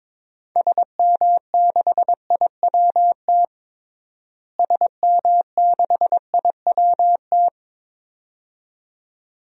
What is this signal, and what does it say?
Morse code, heard as a keyed tone, about 22 wpm, 700 Hz. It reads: SM6IWT SM6IWT
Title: DXCC-100 Call Signs - 6 Characters - Repeated 2x - 22wpm